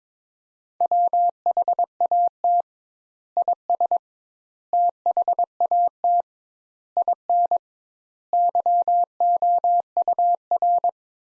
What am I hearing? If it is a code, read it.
Morse code, heard as a keyed tone, about 22 wpm, 700 Hz. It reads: WHAT IS THAT IN YOUR
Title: Sentences Top 200 Words - 22wpm